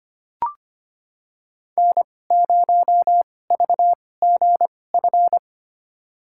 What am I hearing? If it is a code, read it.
Morse code, heard as a keyed tone, about 25 wpm, spaced as for 18 wpm, 700 Hz. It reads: N0VGF
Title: USA Call Signs - 5 Characters - 2x Character Spacing - 25wpm